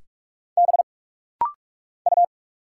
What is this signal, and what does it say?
Morse code, heard as a keyed tone, about 46 wpm, 700 Hz. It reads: B U
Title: Single Letter-Number Alt - Rapid-Fire 45wpm